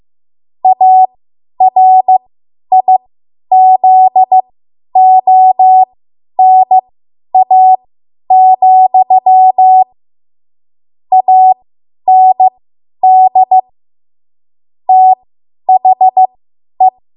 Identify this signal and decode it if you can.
Morse code, heard as a keyed tone, about 15 wpm, spaced as for 10 wpm, 750 Hz. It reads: ARIZONA, AND THE
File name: Morse / 10WPM